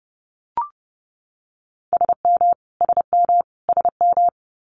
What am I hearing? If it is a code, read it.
Morse code, heard as a keyed tone, about 30 wpm, 700 Hz. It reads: SM SM SM
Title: Multi-letter Phonemes Repeated 3x 30wpm